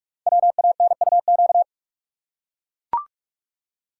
Morse code, heard as a keyed tone, about 44 wpm, 700 Hz. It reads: WANUQ
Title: Code Groups - 5 Letter:Numbers - 45wpm